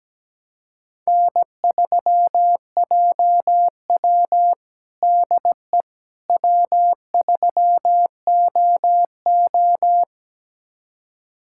Morse code, heard as a keyed tone, about 17 wpm, 700 Hz. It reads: N3JW DE W3OO
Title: USA Callsign DE Callsign - 4 Characters - 17wpm